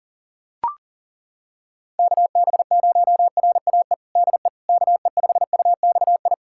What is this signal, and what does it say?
Morse code, heard as a keyed tone, about 40 wpm, 700 Hz. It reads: KB0RAE DE KE5UXI
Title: USA Callsign DE Callsign 40wpm